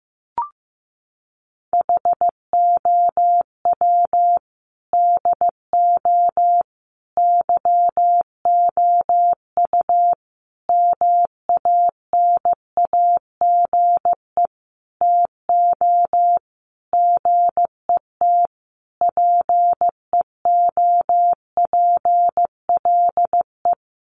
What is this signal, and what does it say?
Morse code, heard as a keyed tone, about 15 wpm, 700 Hz. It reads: HOW DO YOU MANAGE TO GET PEOPLE
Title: Sentences from Top 1400 Words 15wpm